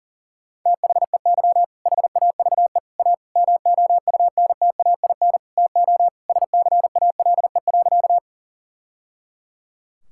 Morse code, heard as a keyed tone, about 40 wpm, 700 Hz. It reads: THEY HAVE A MOUNTAIN TO SCALE.
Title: Sentences from Top 1000 Words - Review Part 5 - 40wpm